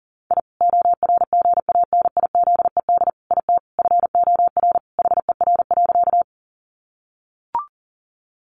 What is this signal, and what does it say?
Morse code, heard as a keyed tone, about 40 wpm, 700 Hz. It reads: I ORGANIZED IT FOR HER.
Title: Sentences from Top 1300 Words 40wpm